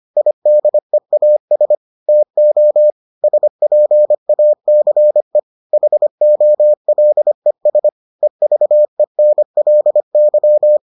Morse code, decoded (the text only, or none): IDEAS TO SPACE HOLES EVENLY